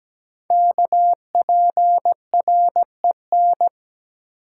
Morse code, heard as a keyed tone, about 17 wpm, 700 Hz. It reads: KPREN
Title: Code Groups - 5 Letters - 17wpm